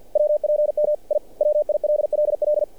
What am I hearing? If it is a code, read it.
Morse code, heard as a keyed tone, about 49 wpm, 600 Hz. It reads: WPM I WILLL
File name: archived / CRSnet